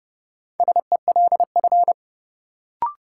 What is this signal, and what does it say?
Morse code, heard as a keyed tone, about 31 wpm, 700 Hz. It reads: SELF
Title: Sentences from Top 1000 Words - Review Part 1 - New Word Drill - 30wpm